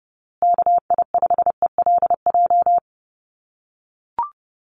Morse code, heard as a keyed tone, about 30 wpm, 700 Hz. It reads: KI5ELJ
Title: USA Call Signs Repeated 2x 30wpm